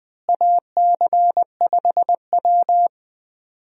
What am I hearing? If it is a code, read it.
Morse code, heard as a keyed tone, about 20 wpm, 700 Hz. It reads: AC5W
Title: USA Call Signs - 4 Characters - Repeated 2x - 20wpm